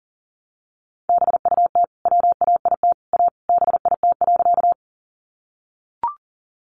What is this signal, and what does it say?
Morse code, heard as a keyed tone, about 40 wpm, 700 Hz. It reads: BUT WAIT A BIT.
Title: Sentences from Top 900 Words 40wpm